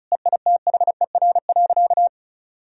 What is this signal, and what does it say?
Morse code, heard as a keyed tone, about 34 wpm, 700 Hz. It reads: EITHER.
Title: Sentences from Top 1000 Words - Review Part 4 - 35wpm